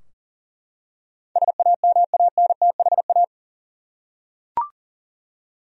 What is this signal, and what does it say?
Morse code, heard as a keyed tone, about 40 wpm, 700 Hz. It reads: SAMANTHA
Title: USA Common Names - 40wpm